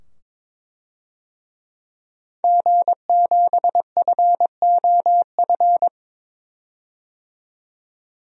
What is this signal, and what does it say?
Morse code, heard as a keyed tone, about 22 wpm, 700 Hz. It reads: G7FOF
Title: DXCC-100 Call Signs - 5 Characters - 22wpm